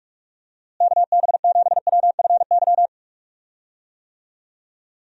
Morse code, heard as a keyed tone, about 46 wpm, 700 Hz. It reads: KB7WFY
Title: USA Call Signs - 6 Characters - Repeated 2x - 45wpm